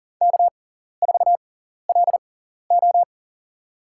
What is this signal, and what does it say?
Morse code, heard as a keyed tone, about 43 wpm, spaced as for 14 wpm, 690 Hz. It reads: K4LO
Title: USA Call Signs - 4 Characters - 6x Character Spacing - 40wpm